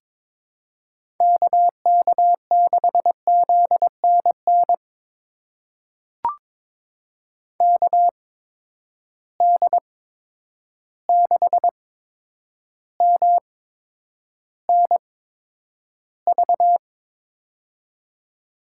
Morse code, heard as a keyed tone, about 22 wpm, spaced as for 9 wpm, 700 Hz. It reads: KK6ZNN KD6MNV
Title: USA Call Signs - 6 Characters - 8x Character Spacing - 22wpm